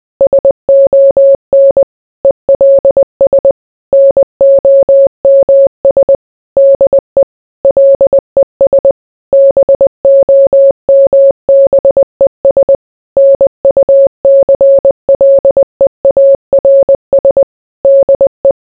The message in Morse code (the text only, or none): SON ELS NOMS DE LES BOMBES NUCLEARS DE